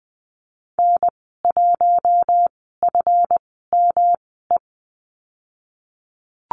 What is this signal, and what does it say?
Morse code, heard as a keyed tone, about 20 wpm, spaced as for 15 wpm, 700 Hz. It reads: N1FME
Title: 193 Lesson 31g - Letter v - Call Signs 20wpm